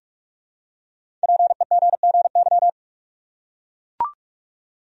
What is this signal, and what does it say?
Morse code, heard as a keyed tone, about 44 wpm, 700 Hz. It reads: PEGGY
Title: USA Common Names - 45wpm